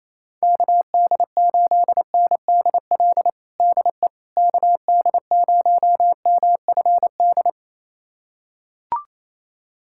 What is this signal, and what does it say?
Morse code, heard as a keyed tone, about 28 wpm, 700 Hz. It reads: KD8NDL DE KD0MFD
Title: USA Callsign DE Callsign 28wpm